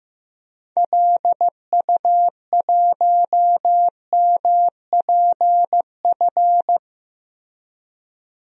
Morse code, heard as a keyed tone, about 15 wpm, 700 Hz. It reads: LU1MPF
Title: DXCC-100 Call Signs - 6 Characters - Repeated 2x - 15wpm